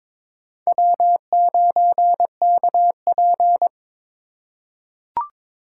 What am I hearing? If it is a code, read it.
Morse code, heard as a keyed tone, about 22 wpm, 700 Hz. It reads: W9KP
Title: USA Call Signs - 4 Characters - 3x Character Spacing - 22wpm